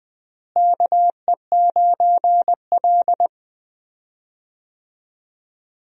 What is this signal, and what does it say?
Morse code, heard as a keyed tone, about 20 wpm, 700 Hz. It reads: KE9L
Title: USA Call Signs - 4 Characters - Repeated 3x - 20wpm